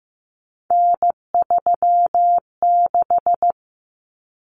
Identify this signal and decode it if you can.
Morse code, heard as a keyed tone, about 15 wpm, 700 Hz. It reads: N36
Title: Common Cut Numbers - 3-Digit 15wpm